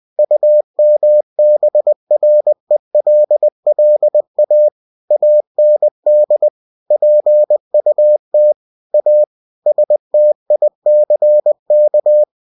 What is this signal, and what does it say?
Morse code, heard as a keyed tone, about 20 wpm, 600 Hz. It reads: UMBRELLA AND PUT A STICK
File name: cw-books / winnie-the-pooh / winnie-20wpm-page-0096